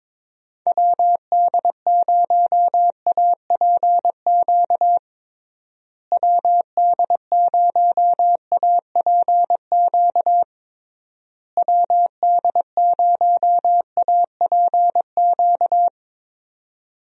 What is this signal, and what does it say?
Morse code, heard as a keyed tone, about 22 wpm, 700 Hz. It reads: WD0APQ WD0APQ WD0APQ
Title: USA Call Signs - 6 Characters - Repeated 3x - 22wpm